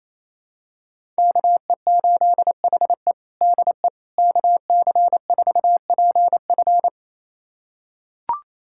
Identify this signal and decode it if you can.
Morse code, heard as a keyed tone, about 28 wpm, 700 Hz. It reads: KE8HE DE KC4PF
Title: USA Callsign DE Callsign - 5 Characters - 28wpm